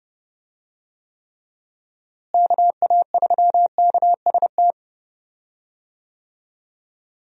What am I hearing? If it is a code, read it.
Morse code, heard as a keyed tone, about 30 wpm, 700 Hz. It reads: KA3KST
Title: USA Call Signs - 6 Characters - Repeated 3x - 30wpm